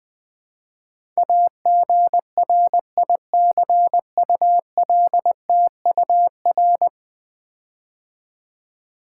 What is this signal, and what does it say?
Morse code, heard as a keyed tone, about 20 wpm, 700 Hz. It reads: AGRICULTUR
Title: Build A Word - 5 Letters At A Time - 12-Letter Words - 20wpm